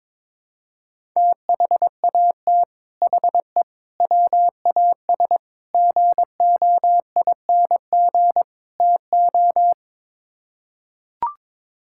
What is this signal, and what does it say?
Morse code, heard as a keyed tone, about 22 wpm, 700 Hz. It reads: THAT HE WAS GOING TO
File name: practice / 22wpm / Build A Phrase - 5-Word Phrases - 22wpm